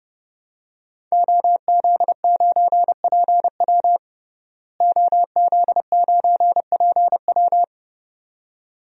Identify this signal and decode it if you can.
Morse code, heard as a keyed tone, about 31 wpm, 700 Hz. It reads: OZ9PW OZ9PW
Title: DXCC-100 Call Signs - 5 Characters - Repeated 2x - 30wpm